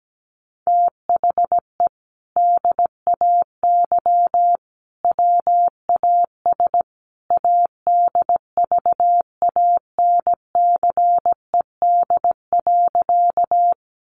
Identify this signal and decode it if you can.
Morse code, heard as a keyed tone, about 17 wpm, 700 Hz. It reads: THE DAY WAS ADVANCED.
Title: Sentences from Top 1200 Words 17wpm